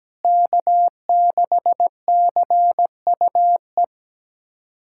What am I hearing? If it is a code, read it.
Morse code, heard as a keyed tone, about 17 wpm, 700 Hz. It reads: K6CUE
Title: USA Callsign DE Callsign - 5 Characters - 17wpm